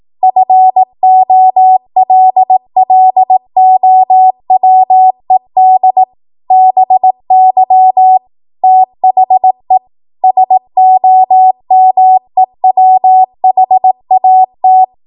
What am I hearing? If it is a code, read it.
Morse code, heard as a keyed tone, about 18 wpm, 750 Hz. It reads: FOLLOWED BY THE SOMEWHAT
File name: Morse / 18WPM